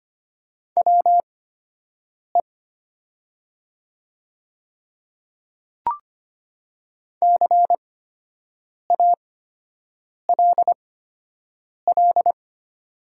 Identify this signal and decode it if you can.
Morse code, heard as a keyed tone, about 25 wpm, spaced as for 7 wpm, 700 Hz. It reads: WE CALL
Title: 8x Character Spacing - Top 300 Words - 25wpm